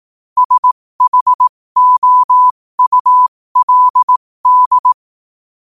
Morse code, heard as a keyed tone, about 18 wpm, spaced as for 15 wpm, 1000 Hz. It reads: SHOULD
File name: qotdfiles / 15wpm